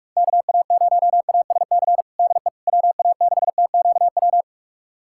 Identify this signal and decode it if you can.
Morse code, heard as a keyed tone, about 42 wpm, 700 Hz. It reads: KA0ASC DE WA6TQW
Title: USA Callsign DE Callsign - 6 Characters - 45wpm